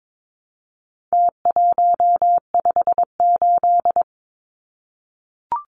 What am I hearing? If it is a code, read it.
Morse code, heard as a keyed tone, about 22 wpm, 700 Hz. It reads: T158
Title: Numbers - Contest Style - 4-Digit 22wpm